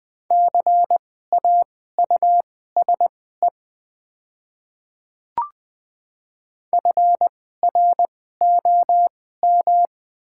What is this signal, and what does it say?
Morse code, heard as a keyed tone, about 20 wpm, spaced as for 14 wpm, 700 Hz. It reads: CAUSE FROM
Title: 2x Character Spacing - Top 200 Words - 20wpm